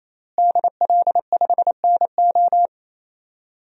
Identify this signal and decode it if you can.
Morse code, heard as a keyed tone, about 28 wpm, 700 Hz. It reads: DL5NO
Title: DXCC-100 Call Signs - 5 Characters - Repeated 3x - 28wpm